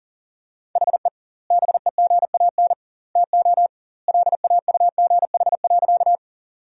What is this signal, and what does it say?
Morse code, heard as a keyed tone, about 37 wpm, 700 Hz. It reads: HE BEGAN TO LAUGH.
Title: 2x Word-Spacing - Sentences from Top 500 Words - 40wpm